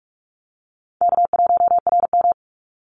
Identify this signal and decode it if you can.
Morse code, heard as a keyed tone, about 44 wpm, 700 Hz. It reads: K1RM
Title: Call Signs Repeated 2x - 45wpm